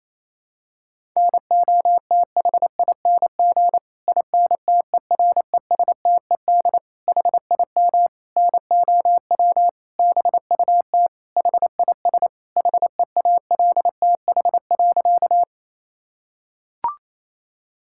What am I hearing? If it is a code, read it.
Morse code, heard as a keyed tone, about 27 wpm, 700 Hz. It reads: NOTHING INTERESTED HIM NOW BUT HIS HEALTH.
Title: Sentences from Top 1000 Words - Review Part 6 - 28wpm